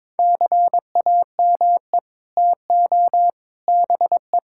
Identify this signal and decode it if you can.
Morse code, heard as a keyed tone, about 22 wpm, 700 Hz. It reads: CAME TO BE
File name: practice / 22wpm / Sets of 6 Words - 22wpm